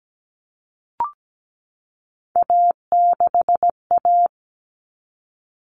Morse code, heard as a keyed tone, about 17 wpm, 700 Hz. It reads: A6A
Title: Common Cut Numbers - 3-Digit 17wpm